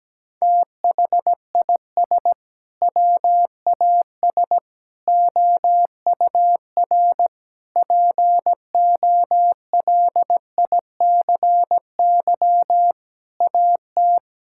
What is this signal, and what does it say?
Morse code, heard as a keyed tone, about 17 wpm, 700 Hz. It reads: THIS WAS OUR POLICY AT
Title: Sentences from Top 1000 Words - Review Part 5 - 17wpm